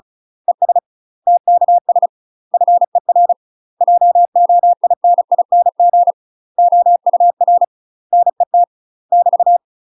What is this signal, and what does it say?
Morse code, heard as a keyed tone, about 36 wpm, 700 Hz. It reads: ES TKS FER JOINING OUR NET <BT>
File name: UK MASTER QSOs TRAINING / 35WPM / UK MASTER QSOs 0032